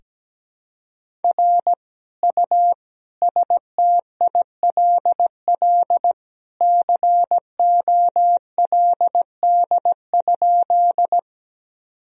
Morse code, heard as a keyed tone, about 17 wpm, 700 Hz. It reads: R U STILL COLD?
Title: Sentences from Top 400 Words and Abbreviations 17wpm